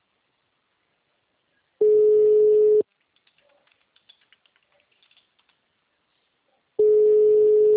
Ringback tone (Europe, 425 Hz).
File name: out- -2053- - - .40201